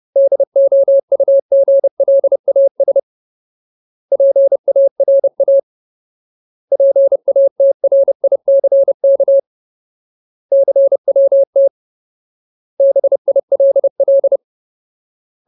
Morse code, heard as a keyed tone, about 30 wpm, 550 Hz. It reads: DOUGLAS PARA PATRICK CWT BILL
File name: Ultimate Word List / 30WPM Ultimate